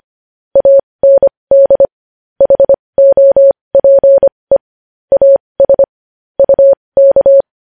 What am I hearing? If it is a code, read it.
Morse code, heard as a keyed tone, about 25 wpm, spaced as for 20 wpm, 550 Hz. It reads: AND HOPE AS UK
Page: News Headlines in Morse Code at 20 WPM